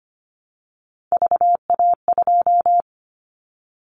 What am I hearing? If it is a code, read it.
Morse code, heard as a keyed tone, about 25 wpm, 700 Hz. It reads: VA2
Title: Callsign Prefixes for Top 100 DXCC - 25wpm